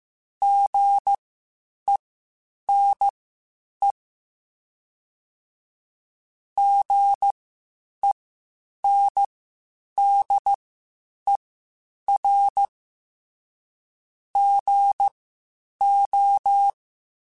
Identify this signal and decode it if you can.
Morse code, heard as a keyed tone, about 15 wpm, spaced as for 8 wpm, 800 Hz. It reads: GENE GENDER GO
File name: Spaced Alphas / Spaced Alphas mp3s / Gs sp @13